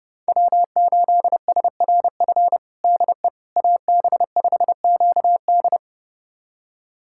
Morse code, heard as a keyed tone, about 30 wpm, 700 Hz. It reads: W8SRF DE AB5QD
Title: USA Callsign DE Callsign - 5 Characters - 30wpm